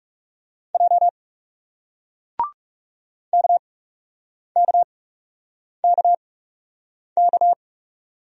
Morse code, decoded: J KKKK

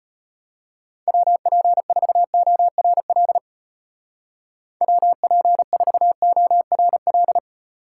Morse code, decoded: WP4ORL WP4ORL

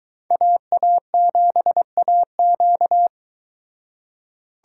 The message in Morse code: AA7AQ

